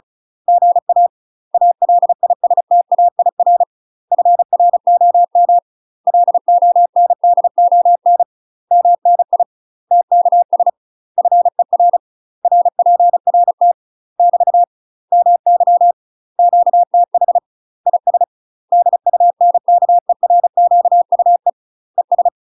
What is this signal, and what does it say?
Morse code, heard as a keyed tone, about 34 wpm, 700 Hz. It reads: GA ALISTAIR FROM LONDON MNI TKS FER RPRT <BT> MY QTH IS DUNKERQUE ES
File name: UK MASTER QSOs TRAINING / 35WPM / UK MASTER QSOs 0025